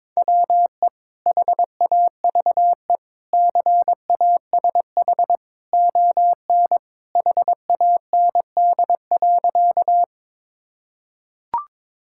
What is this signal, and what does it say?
Morse code, heard as a keyed tone, about 22 wpm, 700 Hz. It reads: WE HAVE CASH ON HAND.
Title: Sentences from Top 1000 Words - Review Part 5 - 22wpm